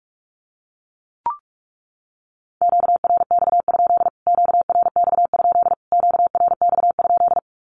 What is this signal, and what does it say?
Morse code, heard as a keyed tone, about 44 wpm, 700 Hz. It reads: QRX? QRX? QRX?
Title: Q-Codes Repeated 3x 45wpm